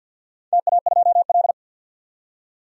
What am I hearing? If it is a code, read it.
Morse code, heard as a keyed tone, about 52 wpm, 700 Hz. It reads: TA2L
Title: DXCC-100 Call Signs - 4 Characters - 50wpm